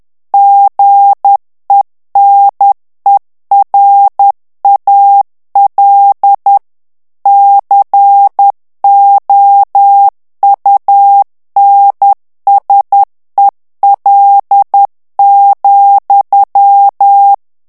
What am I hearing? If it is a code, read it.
Morse code, heard as a keyed tone, about 10 wpm, 800 Hz. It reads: GENERAL COUNSEL,
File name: ANGLAIS texte clair 3 vitesse 10 mots minute